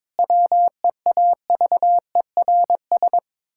Morse code, heard as a keyed tone, about 22 wpm, 700 Hz. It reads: WEAVERS
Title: Pangram Challenge v2 - 22wpm